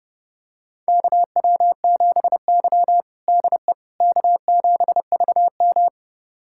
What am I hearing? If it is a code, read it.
Morse code, heard as a keyed tone, about 32 wpm, 700 Hz. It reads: KW7Y DE K7VM
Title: USA Callsign DE Callsign - 4 Characters - 30wpm